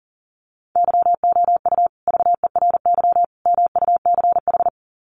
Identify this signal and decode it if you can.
Morse code, heard as a keyed tone, about 40 wpm, 700 Hz. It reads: YOU VERY MUCH
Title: Sets of 3 Words 40wpm